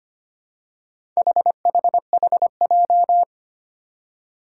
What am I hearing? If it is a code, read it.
Morse code, heard as a keyed tone, about 25 wpm, 700 Hz. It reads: HHHJ
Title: Four Challenging Letter-Number Combinations 25wpm